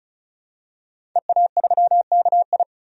Morse code, heard as a keyed tone, about 36 wpm, 700 Hz. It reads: EA3KI
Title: DXCC-100 Call Signs - 5 Characters - 35wpm